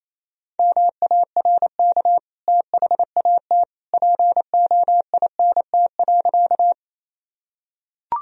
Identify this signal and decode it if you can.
Morse code, heard as a keyed tone, about 28 wpm, 700 Hz. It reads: MARK THAT POINT.
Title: Sentences from Top 1000 Words - Review Part 1 - 28wpm